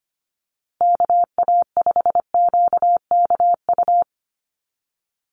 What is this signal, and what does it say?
Morse code, heard as a keyed tone, about 25 wpm, 700 Hz. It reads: KA5QKU
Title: USA Call Signs Repeated 2x 25wpm